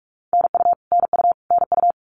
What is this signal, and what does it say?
Morse code, heard as a keyed tone, about 44 wpm, 700 Hz. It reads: NU NU NU
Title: Canadian Provinces and Territories - Repeated 3x - 45wpm